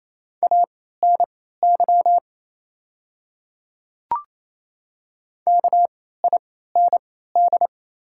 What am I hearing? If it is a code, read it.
Morse code, heard as a keyed tone, about 28 wpm, spaced as for 16 wpm, 700 Hz. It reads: ANY KIND
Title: 3x Character Spacing - Top 200 Words - 28wpm